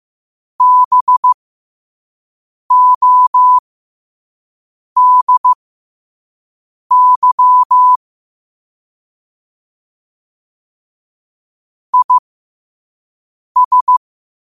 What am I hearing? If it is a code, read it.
Morse code, heard as a keyed tone, about 15 wpm, spaced as for 5 wpm, 1000 Hz. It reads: BODY IS
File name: 5wpm